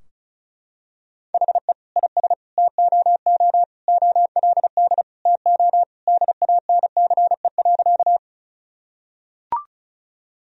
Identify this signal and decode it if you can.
Morse code, heard as a keyed tone, about 36 wpm, 700 Hz. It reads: HE IS TOO OLD TO DANCE.
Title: Sentences from Top 1000 Words - Review Part 4 - 35wpm